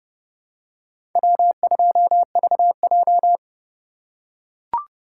Morse code, heard as a keyed tone, about 32 wpm, 700 Hz. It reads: W2VJ
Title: USA Call Signs - 3x Character Spacing - 30wpm